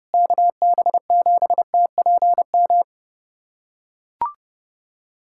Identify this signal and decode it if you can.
Morse code, heard as a keyed tone, about 29 wpm, 700 Hz. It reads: KB7TPM
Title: USA Callsign DE Callsign - 6 Characters - 30wpm